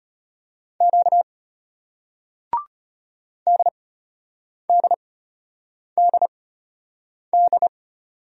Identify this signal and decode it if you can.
Morse code, heard as a keyed tone, about 34 wpm, spaced as for 8 wpm, 700 Hz. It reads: Q DDDD